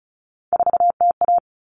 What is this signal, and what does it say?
Morse code, heard as a keyed tone, about 34 wpm, 700 Hz. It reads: 4TA
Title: All Cut Numbers - 3-Digit 35wpm